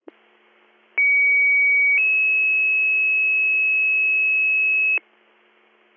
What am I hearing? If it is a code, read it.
Two-tone page, heard as a one-second tone, then a three-second tone, about 2250 then 2600 Hz, with noise. It is heard from 1.0 to 5.0 s.